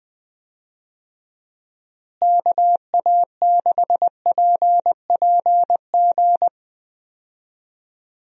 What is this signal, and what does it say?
Morse code, heard as a keyed tone, about 20 wpm, 700 Hz. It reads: KA6PPG
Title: USA Call Signs - 6 Characters - Repeated 2x - 20wpm